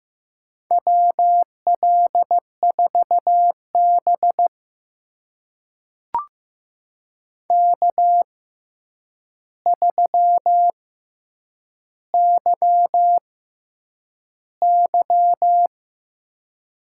Morse code, heard as a keyed tone, about 15 wpm, spaced as for 8 wpm, 700 Hz. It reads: WL4B K3YY
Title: USA Call Signs - 4 Characters - 6x Character Spacing - 15wpm